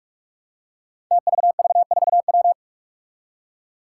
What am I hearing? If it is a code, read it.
Morse code, heard as a keyed tone, about 43 wpm, 700 Hz. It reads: TVV4W
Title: Five Challenging Letter-Number Combinations 45wpm